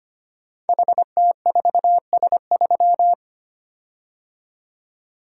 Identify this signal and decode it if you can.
Morse code, heard as a keyed tone, about 25 wpm, 700 Hz. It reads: HT4S3
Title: Code Groups - 5 Letter:Numbers - 25wpm